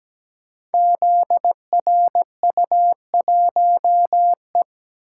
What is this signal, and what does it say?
Morse code, heard as a keyed tone, about 17 wpm, 700 Hz. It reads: ZRU1E
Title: Code Groups - 5 Letter:Numbers - 17wpm